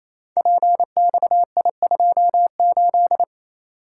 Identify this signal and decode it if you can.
Morse code, heard as a keyed tone, about 28 wpm, 700 Hz. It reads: PXI28
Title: Code Groups - 5 Letter:Numbers - 28wpm